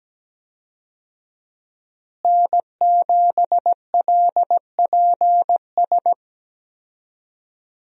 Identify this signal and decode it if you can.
Morse code, heard as a keyed tone, about 17 wpm, 700 Hz. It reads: N7LPS
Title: USA Call Signs - 5 Characters - Repeated 3x - 17wpm